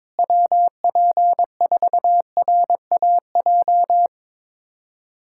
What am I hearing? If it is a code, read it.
Morse code, heard as a keyed tone, about 21 wpm, 700 Hz. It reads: WP4RAJ
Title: USA Callsign DE Callsign - 6 Characters - 22wpm